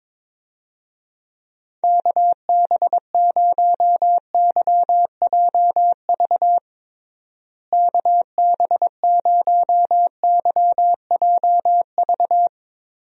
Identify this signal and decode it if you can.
Morse code, heard as a keyed tone, about 22 wpm, 700 Hz. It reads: KB0YJV KB0YJV
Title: USA Call Signs - 6 Characters - Repeated 2x - 22wpm